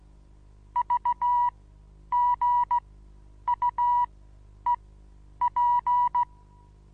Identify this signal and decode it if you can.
Morse code, heard as a keyed tone, about 16 wpm, spaced as for 9 wpm, 970 Hz. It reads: VGUEP
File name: Onlinelehrgang / morsen / DJ4UF Morsen25